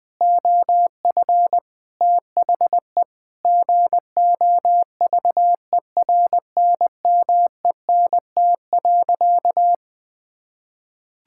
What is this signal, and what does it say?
Morse code, heard as a keyed tone, about 20 wpm, 700 Hz. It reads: OF THE GOVERNMENT.
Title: Sentences from Top 1600 Words 20wpm